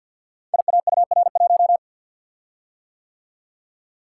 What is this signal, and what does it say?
Morse code, heard as a keyed tone, about 49 wpm, 700 Hz. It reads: IAUR1
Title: Code Groups - 5 Letter:Numbers - 50wpm